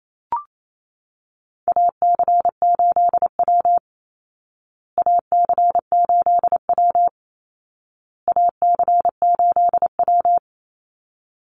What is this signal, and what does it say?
Morse code, heard as a keyed tone, about 28 wpm, 700 Hz. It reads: AC8W AC8W AC8W
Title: Call Signs Repeated 2x - 28wpm